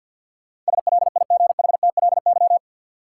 Morse code, heard as a keyed tone, about 49 wpm, 700 Hz. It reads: SLIGHTLY
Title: Sentences from Top 1300 Words - New Word Drill - 50wpm